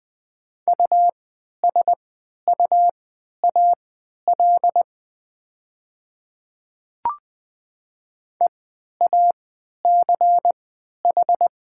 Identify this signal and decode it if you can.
Morse code, heard as a keyed tone, about 20 wpm, spaced as for 11 wpm, 700 Hz. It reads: USUAL EACH
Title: 3x Character Spacing - Top 500 Words - 20wpm